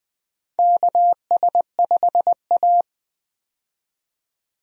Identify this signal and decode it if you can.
Morse code, heard as a keyed tone, about 20 wpm, 700 Hz. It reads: KS5A
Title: USA Call Signs - 4 Characters - 20wpm